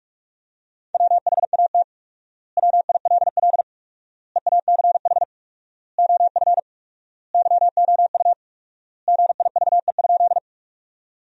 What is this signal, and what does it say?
Morse code, heard as a keyed tone, about 46 wpm, 700 Hz. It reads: WHAT WILL EACH OF YOU GIVE?
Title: 4x Word-Spacing - Sentences from Top 200 Words - 45wpm